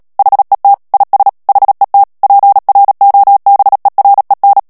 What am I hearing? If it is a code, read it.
Morse code, heard as a keyed tone, about 37 wpm, 800 Hz. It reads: HET IS HET PROBEREN
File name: morse mp3 / HOLLANDAIS texte clair 2 vitesse 35 mots minute